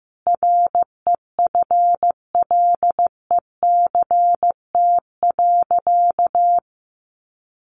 Morse code, heard as a keyed tone, about 15 wpm, 700 Hz. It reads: REFLECT.
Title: Sentences from Top 1500 Words 15wpm